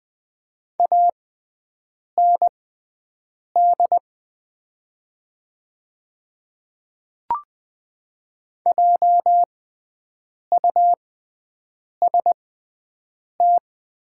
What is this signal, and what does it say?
Morse code, heard as a keyed tone, about 20 wpm, spaced as for 7 wpm, 700 Hz. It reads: AND JUST
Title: 6x Character Spacing - Top 200 Words - 20wpm